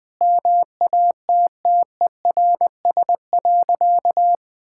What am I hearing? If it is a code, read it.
Morse code, heard as a keyed tone, about 20 wpm, 700 Hz. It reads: MATTERS.
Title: Sentences with 9-Word Phrases - 20wpm